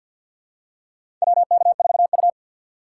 Morse code, heard as a keyed tone, about 52 wpm, 700 Hz. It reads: WK4U
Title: USA Call Signs - 4 Characters - 6x Character Spacing - 50wpm